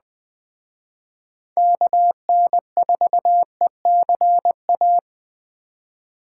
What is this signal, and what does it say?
Morse code, heard as a keyed tone, about 20 wpm, 700 Hz. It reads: KN4ECA
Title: Speed Racing - USA 2x3 Call Signs 17wpm